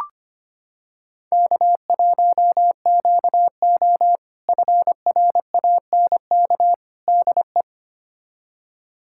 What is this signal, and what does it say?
Morse code, heard as a keyed tone, about 25 wpm, 700 Hz. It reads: K1QO FRANK DE
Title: K1USN SST QSO Practice - 25wpm